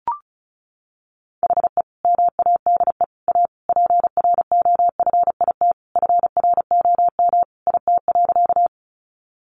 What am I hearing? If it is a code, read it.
Morse code, heard as a keyed tone, about 34 wpm, 700 Hz. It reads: HE MADE A PROFIT FROM IT.
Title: Sentences from Top 800 Words 35wpm